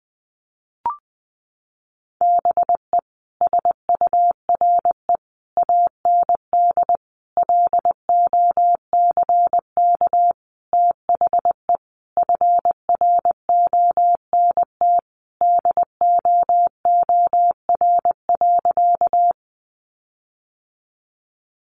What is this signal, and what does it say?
Morse code, heard as a keyed tone, about 20 wpm, 700 Hz. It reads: BE SURE AND LOCK THE FRONT DOOR.
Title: Sentences from Top 1300 Words 20wpm